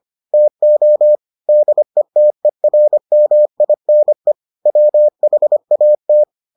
Morse code, heard as a keyed tone, about 25 wpm, 600 Hz. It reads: TO DETERMINE WHAT